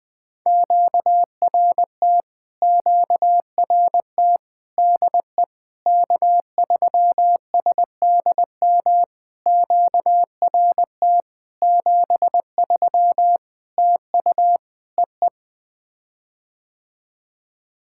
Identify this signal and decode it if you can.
Morse code, heard as a keyed tone, about 20 wpm, 700 Hz. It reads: QRT QRT DE K3SDM QRT 73 TU EE
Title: POTA QSO Practice - 20wpm